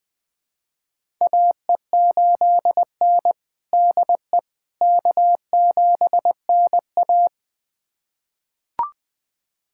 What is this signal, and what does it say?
Morse code, heard as a keyed tone, about 20 wpm, 700 Hz. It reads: AE8N DE K7NA
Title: USA Callsign DE Callsign - 4 Characters - 20wpm